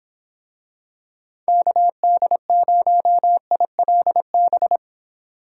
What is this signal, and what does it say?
Morse code, heard as a keyed tone, about 23 wpm, 700 Hz. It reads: KD0ILB